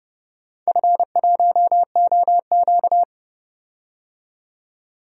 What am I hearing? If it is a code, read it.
Morse code, heard as a keyed tone, about 30 wpm, 700 Hz. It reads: F1OQ
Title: DXCC-100 Call Signs - 4 Characters - 30wpm